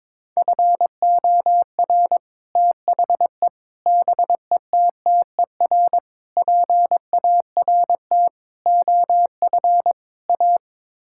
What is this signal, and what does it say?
Morse code, heard as a keyed tone, about 22 wpm, 700 Hz. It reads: FOR THE BETTER PART OF A
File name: practice / 22wpm / Sets of 6 Words - 22wpm